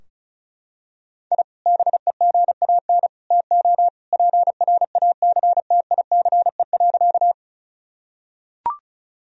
Morse code, heard as a keyed tone, about 36 wpm, 700 Hz. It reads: I BEGAN TO PRACTICE.
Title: Sentences from Top 1000 Words - Review Part 4 - 35wpm